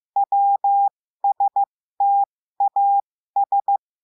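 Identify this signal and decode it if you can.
Morse code, heard as a keyed tone, about 15 wpm, spaced as for 13 wpm, 800 Hz. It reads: WSTAS
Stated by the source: Morse code lesson adding the letter W, with a five-letter-group copy drill.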